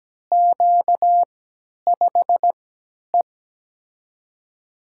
Morse code, heard as a keyed tone, about 17 wpm, 700 Hz. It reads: Q 5 E
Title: USA Call Signs - 3x Character Spacing - 17wpm